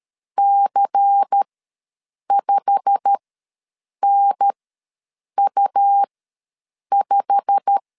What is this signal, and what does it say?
Morse code, heard as a keyed tone, about 13 wpm, spaced as for 7 wpm, 800 Hz. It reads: C5NU5
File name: LEZIONI / 8 YJ45 / 8 YJ45 ws 12wpm 3146